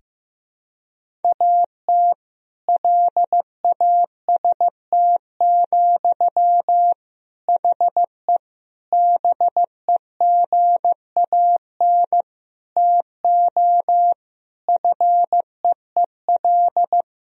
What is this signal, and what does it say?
Morse code, heard as a keyed tone, about 15 wpm, 700 Hz. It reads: AT LAST, HE BEGAN TO FEEL
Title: Sentences from Top 1000 Words - Review Part 1 - 15wpm